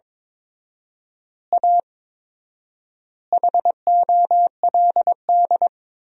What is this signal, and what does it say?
Morse code, heard as a keyed tone, about 22 wpm, 700 Hz. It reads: A HOLD
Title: 4x Word-Spacing - Sentences from Top 400 Words - 22wpm